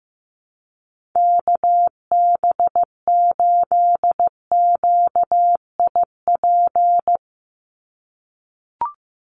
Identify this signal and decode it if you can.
Morse code, heard as a keyed tone, about 15 wpm, 700 Hz. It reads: KB8QIP
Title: USA Call Signs 15wpm